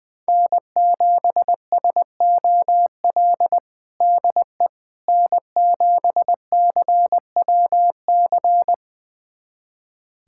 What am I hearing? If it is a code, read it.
Morse code, heard as a keyed tone, about 20 wpm, 700 Hz. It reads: N7SOL DE N7CWC
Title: USA Callsign DE Callsign - 5 Characters - 20wpm